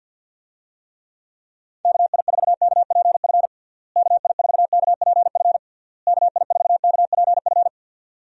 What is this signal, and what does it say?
Morse code, heard as a keyed tone, about 49 wpm, 700 Hz. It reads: KI4KPF KI4KPF KI4KPF
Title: USA Call Signs - 6 Characters - Repeated 3x - 50wpm